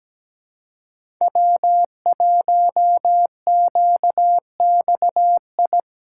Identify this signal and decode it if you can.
Morse code, heard as a keyed tone, about 17 wpm, 700 Hz. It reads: W1QXI
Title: USA Callsign DE Callsign - 5 Characters - 17wpm